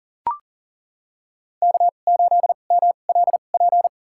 Morse code, heard as a keyed tone, about 40 wpm, spaced as for 29 wpm, 700 Hz. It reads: K8MLP
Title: USA Call Signs - 5 Characters - 2x Character Spacing - 40wpm